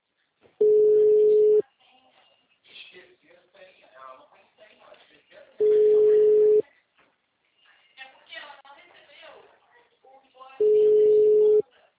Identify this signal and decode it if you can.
Ringback tone (Europe, 425 Hz), with speech.